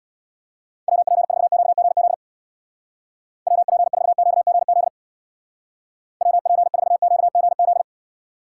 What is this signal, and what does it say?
Morse code, heard as a keyed tone, about 64 wpm, 700 Hz. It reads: WP4QGZ WP4QGZ WP4QGZ